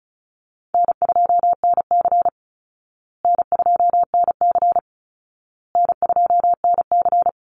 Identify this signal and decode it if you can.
Morse code, heard as a keyed tone, about 34 wpm, 700 Hz. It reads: N2NC N2NC N2NC
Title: Call Signs Repeated 2x - 35wpm